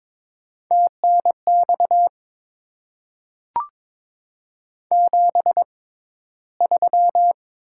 Morse code, heard as a keyed tone, about 22 wpm, spaced as for 13 wpm, 700 Hz. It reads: TNX 73
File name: Top 20 QSO Elements - 6x Character Spacing - 22wpm